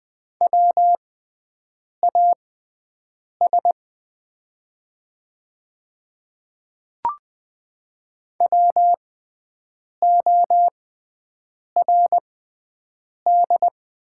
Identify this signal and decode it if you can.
Morse code, heard as a keyed tone, about 20 wpm, spaced as for 7 wpm, 690 Hz. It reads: WAS WORD
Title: 6x Character Spacing - Top 100 Words - 20wpm